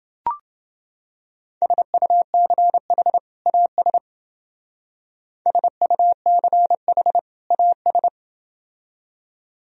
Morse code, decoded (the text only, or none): SUCH AS SUCH AS